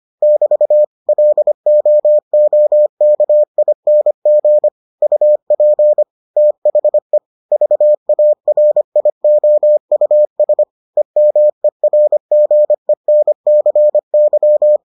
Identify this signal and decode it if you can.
Morse code, heard as a keyed tone, about 25 wpm, 600 Hz. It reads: <BT> LOOKING UP THE VARIOUS EMERGENCY